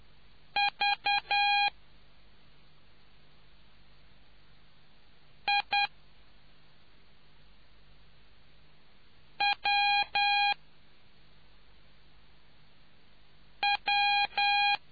Morse code, heard as a keyed tone, about 10 wpm, 810 Hz. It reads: V I W W